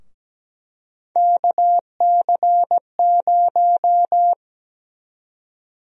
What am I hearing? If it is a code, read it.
Morse code, heard as a keyed tone, about 17 wpm, 700 Hz. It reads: KC0
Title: USA Call Sign Prefixes - Repeated 2x - 17wpm